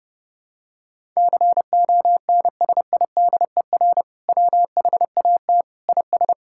Morse code, heard as a keyed tone, about 30 wpm, 700 Hz. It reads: CONSIDER WHAT IS
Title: Sentences from Top 1000 Words - Review Part 5 - 30wpm